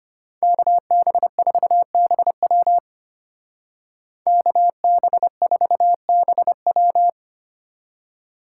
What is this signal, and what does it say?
Morse code, heard as a keyed tone, about 32 wpm, 700 Hz. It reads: KB4BW KB4BW